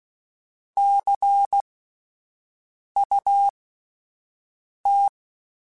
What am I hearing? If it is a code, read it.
Morse code, heard as a keyed tone, about 16 wpm, spaced as for 5 wpm, 800 Hz. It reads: CUT
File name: Misc / Misc-mp3s / WordPairs sp @10